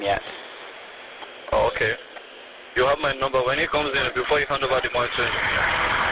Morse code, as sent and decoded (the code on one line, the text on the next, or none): none